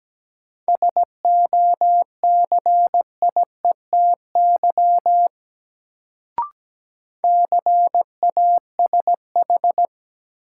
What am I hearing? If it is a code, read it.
Morse code, heard as a keyed tone, about 17 wpm, 700 Hz. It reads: SOCIETY CASH
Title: Sentences from Top 700 Words - New Word Drill - 17wpm